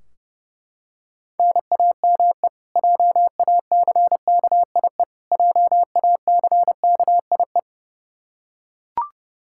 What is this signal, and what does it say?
Morse code, heard as a keyed tone, about 30 wpm, 700 Hz. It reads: NAME JACKIE JACKIE
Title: Practice QSO - Exchange 1 - Beginning - 30wpm